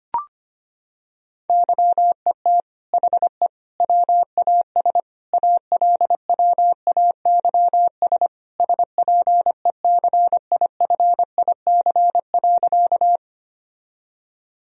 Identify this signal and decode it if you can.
Morse code, heard as a keyed tone, about 25 wpm, 700 Hz. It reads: YET HE WAS ALWAYS SPECIFIC.
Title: Sentences from Top 1000 Words - Review Part 3 - 25wpm